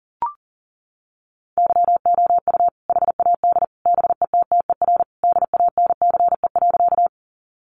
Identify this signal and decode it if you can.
Morse code, heard as a keyed tone, about 37 wpm, 700 Hz. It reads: YOU HAD BETTER DANCE.
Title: Sentences from Top 700 Words 40wpm